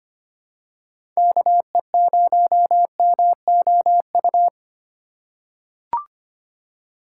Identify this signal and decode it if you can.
Morse code, heard as a keyed tone, about 25 wpm, 700 Hz. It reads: KE0MOU
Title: USA Call Signs - 6 Characters - 3x Character Spacing - 25wpm